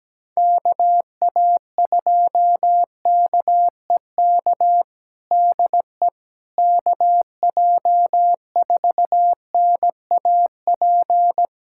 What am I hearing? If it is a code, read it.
Morse code, heard as a keyed tone, about 17 wpm, 700 Hz. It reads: KA2KEK DE KJ4NAP
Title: USA Callsign DE Callsign - 6 Characters - 17wpm